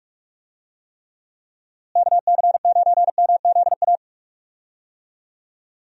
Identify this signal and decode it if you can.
Morse code, heard as a keyed tone, about 45 wpm, 700 Hz. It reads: KC9MZA
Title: USA Call Signs - 6 Characters - Repeated 3x - 45wpm